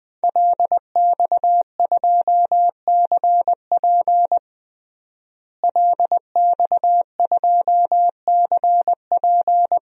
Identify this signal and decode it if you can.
Morse code, heard as a keyed tone, about 21 wpm, 700 Hz. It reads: LX2CP LX2CP
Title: DXCC-100 Call Signs - 5 Characters - Repeated 2x - 20wpm